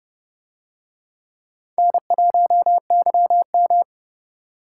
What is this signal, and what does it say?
Morse code, heard as a keyed tone, about 30 wpm, 700 Hz. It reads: N1YM